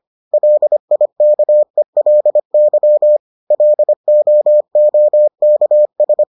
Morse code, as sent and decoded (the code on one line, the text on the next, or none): .-.. .. -.- . .-.. -.-- / .-.. --- --- -.- ...
LIKELY LOOKS